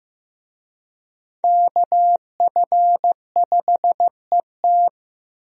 Morse code, heard as a keyed tone, about 14 wpm, 700 Hz. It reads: KF5ET
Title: USA Call Signs - 5 Characters - Repeated 2x - 15wpm